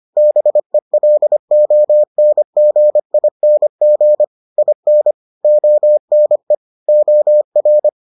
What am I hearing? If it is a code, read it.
Morse code, heard as a keyed tone, about 25 wpm, 600 Hz. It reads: BELONGING IN ONE OR